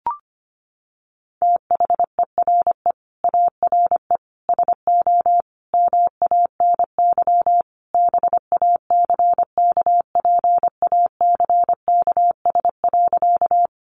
Morse code, heard as a keyed tone, about 25 wpm, 700 Hz. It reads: THERE ARE SO MANY BACKPACKS.
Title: Additional Easy Phrases 25wpm